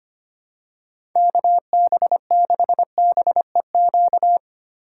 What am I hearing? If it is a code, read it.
Morse code, heard as a keyed tone, about 25 wpm, 700 Hz. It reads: KB6BEQ
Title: USA Call Signs - 6 Characters - 6x Character Spacing - 25wpm